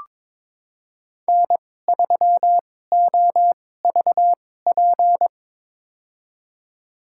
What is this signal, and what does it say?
Morse code, heard as a keyed tone, about 22 wpm, spaced as for 16 wpm, 700 Hz. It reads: N3OVP
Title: USA Call Signs - 5 Characters - 2x Character Spacing - 22wpm